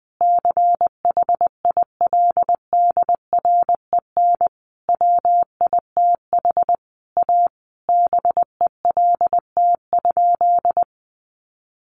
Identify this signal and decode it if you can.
Morse code, heard as a keyed tone, about 20 wpm, 700 Hz. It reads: CHILDREN WITH A BELT?
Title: Sentences from Top 1400 Words 20wpm